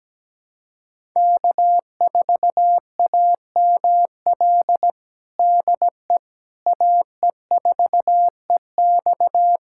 Morse code, heard as a keyed tone, about 17 wpm, 700 Hz. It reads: K4AML DE AE4EX
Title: USA Callsign DE Callsign - 5 Characters - 17wpm